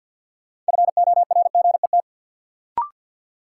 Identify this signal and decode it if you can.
Morse code, heard as a keyed tone, about 49 wpm, 700 Hz. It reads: FORGET
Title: Sentences from Top 1200 Words - New Word Drill Alt - 50wpm